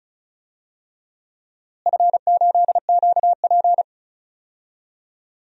Morse code, heard as a keyed tone, about 34 wpm, 700 Hz. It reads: F8QP
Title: DXCC-100 Call Signs - 4 Characters - Repeated 3x - 35wpm